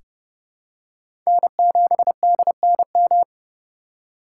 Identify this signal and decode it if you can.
Morse code, heard as a keyed tone, about 29 wpm, 700 Hz. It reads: N7DNM
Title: USA Call Signs - 5 Characters - 4x Character Spacing - 30wpm